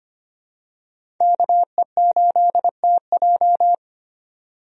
Morse code, heard as a keyed tone, about 25 wpm, 700 Hz. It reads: KE8TJ
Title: USA Call Signs - 5 Characters - 25wpm